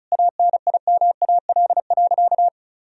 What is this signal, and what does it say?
Morse code, heard as a keyed tone, about 34 wpm, 700 Hz. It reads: ANIMAL.
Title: Sentences from Top 1000 Words - Review Part 2 - 35wpm